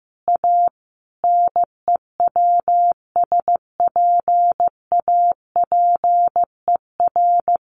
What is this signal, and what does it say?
Morse code, heard as a keyed tone, about 15 wpm, 700 Hz. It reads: A NEWSPAPER
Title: Sentences from Top 1100 Words 15wpm